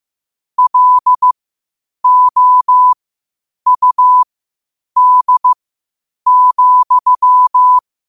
Morse code, heard as a keyed tone, about 15 wpm, spaced as for 8 wpm, 1000 Hz. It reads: LOUD,
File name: qotdfiles / 8wpm